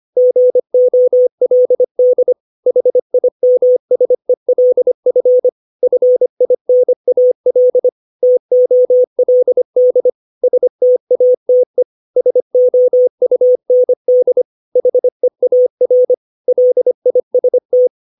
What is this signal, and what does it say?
Morse code, heard as a keyed tone, about 25 wpm, 500 Hz. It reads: GOLD HIMSELF FINAL TOLD STATE SOUND HEAR LIST
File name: 632 words / 632 Words 25WPM